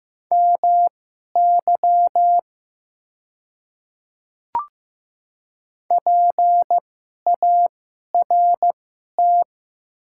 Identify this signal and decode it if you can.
Morse code, heard as a keyed tone, about 15 wpm, spaced as for 11 wpm, 700 Hz. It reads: MY PART
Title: 2x Character Spacing - Top 300 Words - 15wpm